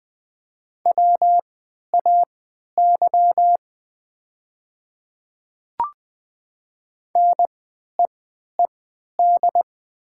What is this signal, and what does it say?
Morse code, heard as a keyed tone, about 20 wpm, spaced as for 11 wpm, 700 Hz. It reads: WAY NEED
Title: 3x Character Spacing - Top 200 Words - 20wpm